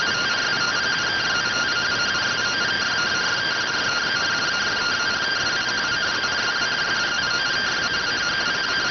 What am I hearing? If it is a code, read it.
Two-tone teleprinter signal, about 170 Hz shift.
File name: findings / CT SAGITAL 27.gen-18 rx